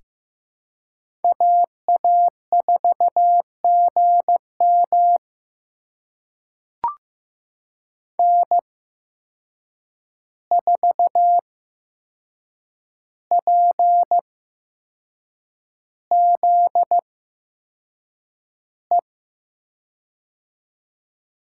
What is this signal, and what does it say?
Morse code, heard as a keyed tone, about 15 wpm, spaced as for 6 wpm, 700 Hz. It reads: AA4GM N4PZE
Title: USA Call Signs - 5 Characters - 8x Character Spacing - 15wpm